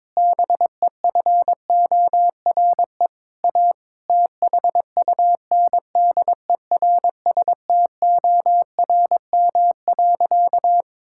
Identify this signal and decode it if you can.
Morse code, heard as a keyed tone, about 22 wpm, 700 Hz. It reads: BEFORE A THUNDERSTORM.
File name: Difficult Sentences with Hints - 22wpm